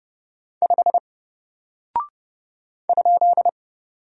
Morse code, heard as a keyed tone, about 31 wpm, 700 Hz. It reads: 5 ?
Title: Single Character (Letter:Number:Punctuation:Prosign) Alt 30wpm